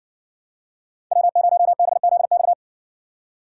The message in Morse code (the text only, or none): W0BZX